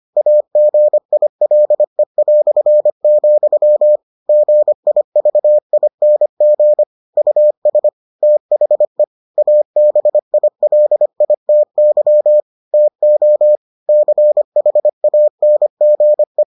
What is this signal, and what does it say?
Morse code, heard as a keyed tone, about 25 wpm, 600 Hz. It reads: AGILE", GIVING US THE ABILITY TO CHANGE